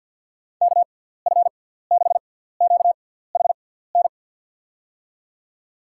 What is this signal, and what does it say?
Morse code, heard as a keyed tone, about 52 wpm, spaced as for 17 wpm, 700 Hz. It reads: KF6QHN
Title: USA Call Signs - 6 Characters - 6x Character Spacing - 50wpm